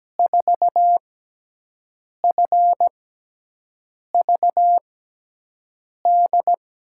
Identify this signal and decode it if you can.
Morse code, heard as a keyed tone, about 17 wpm, 700 Hz. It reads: 4 F V D
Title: USA Call Signs - 6 Characters - 6x Character Spacing - 17wpm